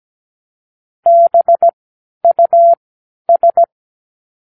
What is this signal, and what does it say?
Morse code, heard as a keyed tone, about 17 wpm, spaced as for 10 wpm, 680 Hz. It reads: BUS
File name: k7qo / Practicewords